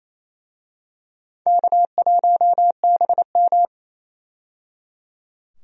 Morse code, heard as a keyed tone, about 27 wpm, 700 Hz. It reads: K1BM
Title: USA Call Signs - 4 Characters - 28wpm